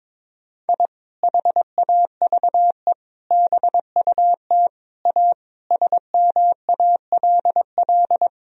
Morse code, heard as a keyed tone, about 22 wpm, 700 Hz. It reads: I HAVE BUT A SMALL
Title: Sentences from Top 1000 Words - Review Part 4 - 22wpm